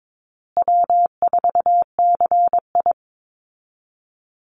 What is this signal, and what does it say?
Morse code, heard as a keyed tone, about 22 wpm, 700 Hz. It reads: W4CI
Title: Call Signs Repeated 3x - 22wpm